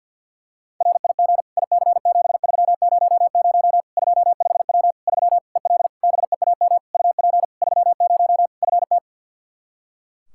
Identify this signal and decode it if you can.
Morse code, heard as a keyed tone, about 48 wpm, 700 Hz. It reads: RIG IC7300 25W 3 EL BEAM UP 30 FT
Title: Practice QSO - Exchange 2 - Info - 50wpm